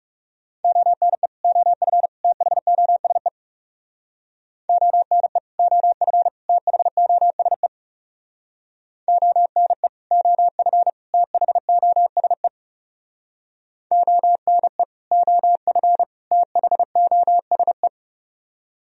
Morse code, heard as a keyed tone, about 39 wpm, 700 Hz. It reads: ONE OF THOSE ONE OF THOSE ONE OF THOSE ONE OF THOSE